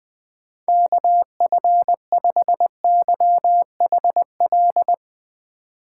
Morse code, heard as a keyed tone, about 20 wpm, 700 Hz. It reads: KF5YHL
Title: USA Call Signs - 6 Characters - 2x Character Spacing - 20wpm